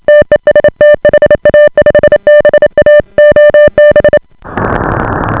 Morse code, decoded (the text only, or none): TEST HA5BA OB